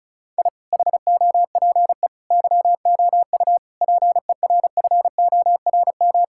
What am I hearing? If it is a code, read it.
Morse code, heard as a keyed tone, about 36 wpm, 700 Hz. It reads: I HOPE YOU PERFORM
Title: Sentences from Top 1000 Words - Review Part 6 - 35wpm